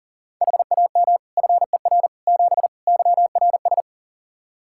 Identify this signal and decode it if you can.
Morse code, heard as a keyed tone, about 40 wpm, 700 Hz. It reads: HAM FER 7 YRS
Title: Practice QSO - Exchange 3 - More Info - 40wpm